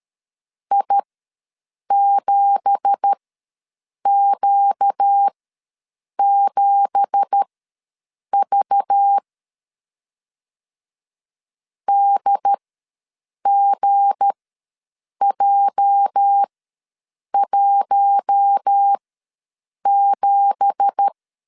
Morse code, heard as a keyed tone, about 13 wpm, spaced as for 7 wpm, 800 Hz. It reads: I7Q7V DGJ17
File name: LEZIONI / 9 2867 ws 12wpm 3431